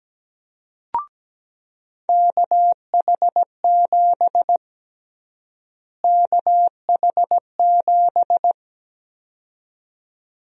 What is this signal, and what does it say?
Morse code, heard as a keyed tone, about 17 wpm, 700 Hz. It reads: KH7 KH7
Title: USA Call Sign Prefixes - Repeated 2x - 17wpm